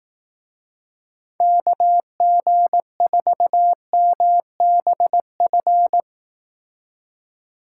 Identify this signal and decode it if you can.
Morse code, heard as a keyed tone, about 19 wpm, 700 Hz. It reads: KG4MBF
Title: Speed Racing - USA 2x3 Call Signs 15wpm